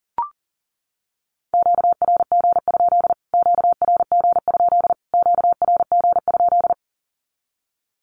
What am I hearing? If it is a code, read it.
Morse code, heard as a keyed tone, about 40 wpm, 700 Hz. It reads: QRG? QRG? QRG?
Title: Q-Codes Repeated 3x 40wpm